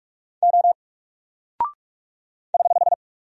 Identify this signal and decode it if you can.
Morse code, heard as a keyed tone, about 44 wpm, 700 Hz. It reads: O <HH>
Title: Single Character (Letter:Number:Punctuation:Prosign) 45wpm